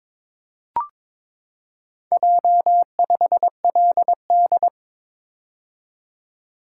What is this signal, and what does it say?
Morse code, heard as a keyed tone, about 22 wpm, 700 Hz. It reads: J5LD
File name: practice / 22wpm / Four Letter-Number Combinations - 22wpm